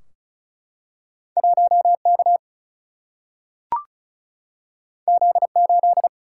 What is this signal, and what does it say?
Morse code, heard as a keyed tone, about 36 wpm, 700 Hz. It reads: 1K Z8
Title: Single-Letter Single-Number 35wpm